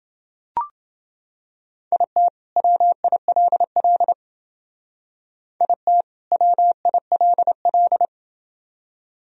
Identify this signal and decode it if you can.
Morse code, heard as a keyed tone, about 28 wpm, 700 Hz. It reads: IT WILL IT WILL